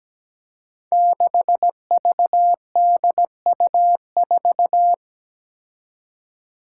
Morse code, heard as a keyed tone, about 17 wpm, 700 Hz. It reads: 6VDU4
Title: Five Challenging Letter-Number Combinations 17wpm